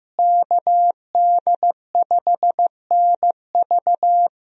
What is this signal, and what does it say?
Morse code, heard as a keyed tone, about 15 wpm, 700 Hz. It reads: KD5NV
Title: USA Call Signs - 5 Characters - Repeated 3x - 15wpm